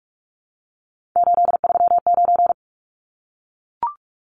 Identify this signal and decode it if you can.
Morse code, heard as a keyed tone, about 44 wpm, 700 Hz. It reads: 839
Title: Numbers - 3-Digit 45wpm